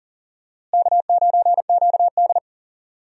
Morse code, heard as a keyed tone, about 39 wpm, 700 Hz. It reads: K9QD
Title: USA Call Signs - 4 Characters - Repeated 2x - 40wpm